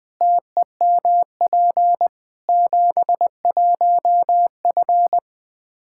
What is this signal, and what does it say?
Morse code, heard as a keyed tone, about 20 wpm, 700 Hz. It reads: TEMP 71F
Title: K1USN SST QSO Practice 20wpm